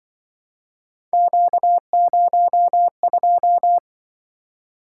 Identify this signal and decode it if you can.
Morse code, heard as a keyed tone, about 24 wpm, 700 Hz. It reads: Q02
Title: Speed-Racing - Three Letter-Number Combinations - 20wpm